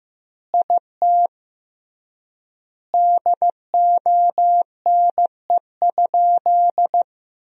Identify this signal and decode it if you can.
Morse code, heard as a keyed tone, about 15 wpm, 700 Hz. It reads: IT DONE?
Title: 3x Word-Spacing - Sentences from Top 500 Words - 15wpm